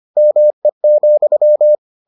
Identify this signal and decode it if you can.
Morse code, heard as a keyed tone, about 25 wpm, 600 Hz. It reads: ME,